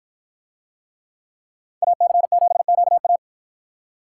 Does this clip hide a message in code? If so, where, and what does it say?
Yes: Morse code from 1.8 to 3.2 s, heard as a keyed tone, about 51 wpm, 700 Hz. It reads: AC7QA